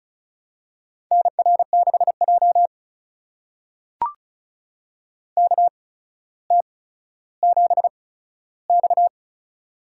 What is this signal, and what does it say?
Morse code, heard as a keyed tone, about 36 wpm, spaced as for 15 wpm, 700 Hz. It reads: NR6J KT7X
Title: USA Call Signs - 4 Characters - 8x Character Spacing - 35wpm